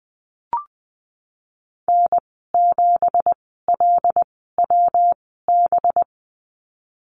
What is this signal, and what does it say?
Morse code, heard as a keyed tone, about 20 wpm, spaced as for 15 wpm, 700 Hz. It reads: N7LWB
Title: 186 Lesson 30g - Letter b - Call Signs 20wpm